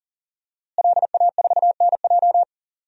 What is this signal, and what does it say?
Morse code, heard as a keyed tone, about 39 wpm, 700 Hz. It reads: LA4NJ
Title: DXCC-100 Call Signs - 5 Characters - Repeated 2x - 40wpm